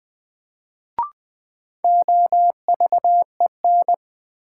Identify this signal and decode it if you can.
Morse code, heard as a keyed tone, about 21 wpm, 700 Hz. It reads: OVEN
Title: Sentences from Top 1000 Words - Review Part 4 - New Word Drill - 20wpm